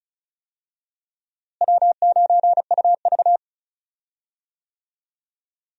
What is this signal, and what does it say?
Morse code, heard as a keyed tone, about 36 wpm, 700 Hz. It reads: W9UV
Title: USA Call Signs - 4 Characters - Repeated 2x - 35wpm